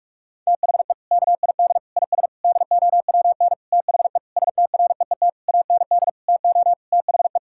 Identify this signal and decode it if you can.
Morse code, heard as a keyed tone, about 46 wpm, 700 Hz. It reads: THE KID IS DOWN THE STREET AND TO THE
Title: Sentences from Top 1600 Words 45wpm